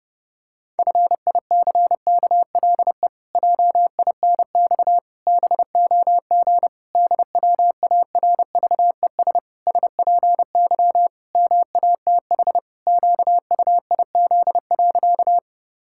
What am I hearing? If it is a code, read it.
Morse code, heard as a keyed tone, about 32 wpm, 700 Hz. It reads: FICKLE JINX BOG DWARVES SPY MATH QUIZ.
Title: Pangram Challenge v2 - 30wpm